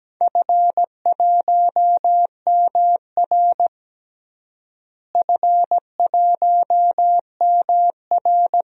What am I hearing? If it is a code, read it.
Morse code, heard as a keyed tone, about 17 wpm, 700 Hz. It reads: F1MR F1MR
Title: DXCC-100 Call Signs - 4 Characters - Repeated 2x - 17wpm